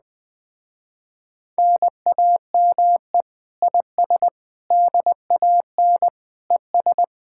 Morse code, heard as a keyed tone, about 20 wpm, 700 Hz. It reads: NAME IS DAN ES
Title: Speed-Racing Common QSO Phrases - 17wpm